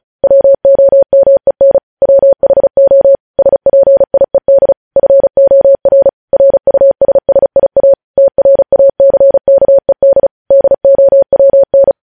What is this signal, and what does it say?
Morse code, heard as a keyed tone, about 36 wpm, 550 Hz. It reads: WOMEN WHO SPIED FOR RUSSIA TRACKED DOWN
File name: podcast / News headlines-1x-35x35